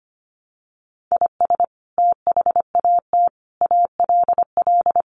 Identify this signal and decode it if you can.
Morse code, heard as a keyed tone, about 25 wpm, 700 Hz. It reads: IS THAT ALL
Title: Sentences from Top 200 Words 25wpm